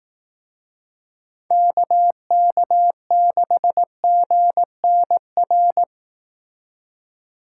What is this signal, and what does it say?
Morse code, heard as a keyed tone, about 18 wpm, 700 Hz. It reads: KK6GNR
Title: Speed Racing - USA 2x3 Call Signs 15wpm